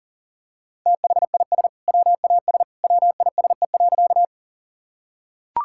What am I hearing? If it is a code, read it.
Morse code, heard as a keyed tone, about 40 wpm, 700 Hz. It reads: THIS WAS WISE.
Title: Sentences from Top 1600 Words 40wpm